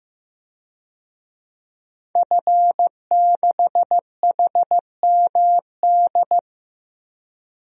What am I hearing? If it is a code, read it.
Morse code, heard as a keyed tone, about 15 wpm, 700 Hz. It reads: F6HMD
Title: DXCC-100 Call Signs - 5 Characters - 15wpm